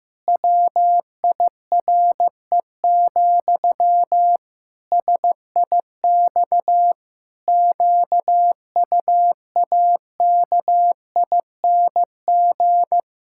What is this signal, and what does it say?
Morse code, heard as a keyed tone, about 15 wpm, 700 Hz. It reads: WIRE, SIX QUAKING
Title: Pangram Challenge v2 - 15wpm